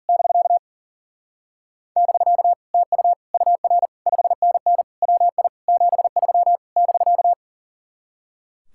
Morse code, heard as a keyed tone, about 48 wpm, 700 Hz. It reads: <BK> <BK> TU UR 5NN WI 73 <BK>
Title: Speed-Racing Common QSO Phrases Encore - 40wpm